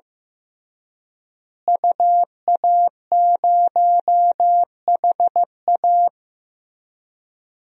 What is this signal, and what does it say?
Morse code, heard as a keyed tone, about 15 wpm, 700 Hz. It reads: UA0HA
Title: DXCC-100 Call Signs - 5 Characters - Repeated 3x - 15wpm